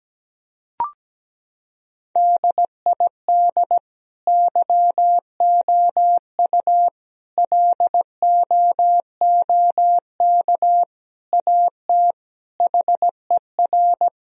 Morse code, heard as a keyed tone, about 17 wpm, 700 Hz. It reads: DID YOU LOOK AT HER
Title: Sentences from Top 1000 Words - Review Part 4 - 17wpm